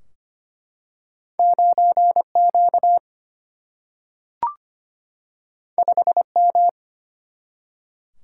Morse code, heard as a keyed tone, about 25 wpm, 700 Hz. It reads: 9 Q 5 M
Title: Single-Letter Single-Number 25wpm